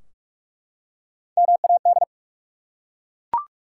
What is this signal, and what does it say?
Morse code, heard as a keyed tone, about 47 wpm, 700 Hz. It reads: MAD
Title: Three Letter Words - 45wpm